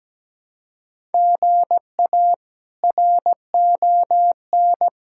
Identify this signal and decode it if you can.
Morse code, heard as a keyed tone, about 17 wpm, 700 Hz. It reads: GA RON
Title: K1USN SST QSO Practice - 17wpm